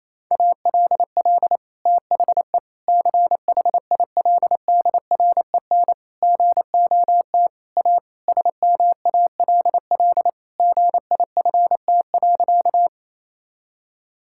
Morse code, heard as a keyed tone, about 27 wpm, 700 Hz. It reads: ALL THE CHILDREN GOT A SMALL GIFT.
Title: Sentences from Top 1000 Words - Review Part 6 - 28wpm